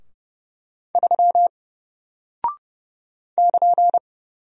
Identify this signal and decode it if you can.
Morse code, heard as a keyed tone, about 30 wpm, 700 Hz. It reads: 3 <KN>
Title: Single Character (Letter:Number:Punctuation:Prosign) Alt 30wpm